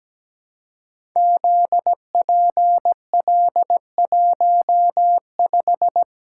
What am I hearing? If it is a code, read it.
Morse code, heard as a keyed tone, about 17 wpm, 700 Hz. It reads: ZPL15
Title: Code Groups - 5 Letter:Numbers - 17wpm